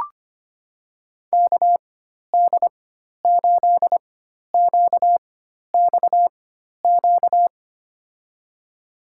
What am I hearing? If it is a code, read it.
Morse code, heard as a keyed tone, about 25 wpm, spaced as for 12 wpm, 700 Hz. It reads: KD8QXQ
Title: USA Call Signs - 6 Characters - 4x Character Spacing - 25wpm